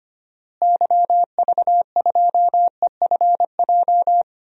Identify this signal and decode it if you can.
Morse code, heard as a keyed tone, about 25 wpm, 700 Hz. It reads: YV2EFJ
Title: DXCC-100 Call Signs - 6 Characters - Repeated 2x - 25wpm